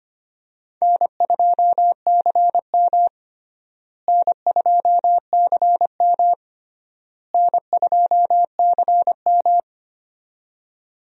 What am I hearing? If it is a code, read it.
Morse code, heard as a keyed tone, about 25 wpm, 700 Hz. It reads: N2CM N2CM N2CM
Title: USA Call Signs - 4 Characters - Repeated 3x - 25wpm